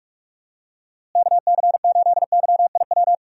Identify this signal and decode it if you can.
Morse code, heard as a keyed tone, about 46 wpm, 700 Hz. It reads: KC8YIW
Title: USA Call Signs - 6 Characters - Repeated 3x - 45wpm